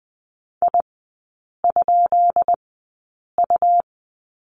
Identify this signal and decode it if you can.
Morse code, heard as a keyed tone, about 20 wpm, spaced as for 8 wpm, 700 Hz. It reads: I?U
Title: 166 Lesson 27e - Number 7 - Three Characters 20wpm